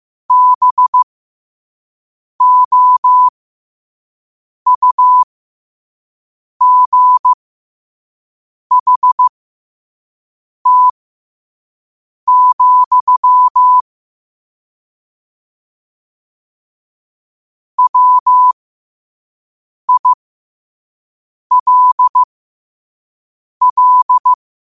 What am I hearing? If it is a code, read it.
Morse code, heard as a keyed tone, about 15 wpm, spaced as for 5 wpm, 1000 Hz. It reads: BOUGHT, WILL
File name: qotdfiles / 5wpm